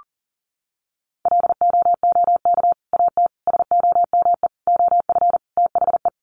Morse code, heard as a keyed tone, about 40 wpm, 700 Hz. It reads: LOOK AT SOME OF THE
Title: Sets of 5 Words Encore 40wpm